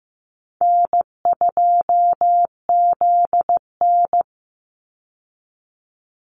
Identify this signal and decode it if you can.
Morse code, heard as a keyed tone, about 15 wpm, 700 Hz. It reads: N2ZN
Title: Easy Call Signs Repeated 3x - 15wpm